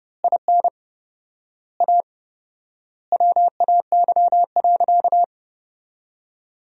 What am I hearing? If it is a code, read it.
Morse code, heard as a keyed tone, about 30 wpm, 700 Hz. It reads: IN A WAY.
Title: 4x Word-Spacing - Sentences from Top 500 Words - 30wpm